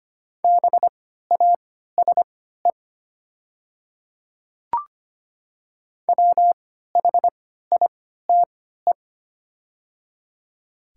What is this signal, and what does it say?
Morse code, heard as a keyed tone, about 25 wpm, spaced as for 14 wpm, 700 Hz. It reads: BASE WHITE
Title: 3x Character Spacing - Top 500 Words - 25wpm